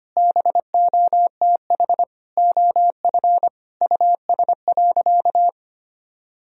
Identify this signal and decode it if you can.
Morse code, heard as a keyed tone, about 25 wpm, 700 Hz. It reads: BOTH OF US.
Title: Sentences from Top 1000 Words - Review Part 1 - 25wpm